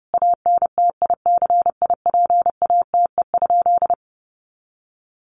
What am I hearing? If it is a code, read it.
Morse code, heard as a keyed tone, about 30 wpm, 700 Hz. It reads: ANTICIPATE?
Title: Sentences from Top 1500 Words 30wpm